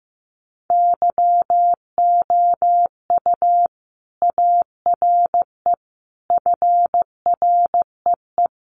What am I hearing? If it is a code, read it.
Morse code, heard as a keyed tone, about 15 wpm, 700 Hz. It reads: YOU ARE FREE
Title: Sentences from Top 700 Words 15wpm